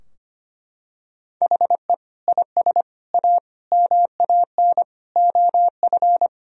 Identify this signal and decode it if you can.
Morse code, heard as a keyed tone, about 25 wpm, 700 Hz. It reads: HE IS A MAN OF
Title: Sentences from Top 1000 Words - Review Part 2 - 25wpm